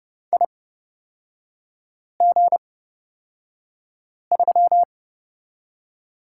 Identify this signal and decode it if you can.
Morse code, heard as a keyed tone, about 29 wpm, spaced as for 5 wpm, 690 Hz. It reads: IG3
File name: Lesson 38e - Letter q - Three Characters 30wpm